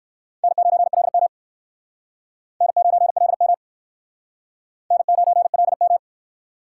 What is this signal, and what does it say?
Morse code, heard as a keyed tone, about 68 wpm, 700 Hz. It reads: N9LM N9LM N9LM